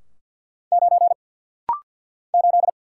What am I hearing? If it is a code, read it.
Morse code, heard as a keyed tone, about 50 wpm, 700 Hz. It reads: 9 8
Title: Single Letter-Number - Rapid-Fire 50wpm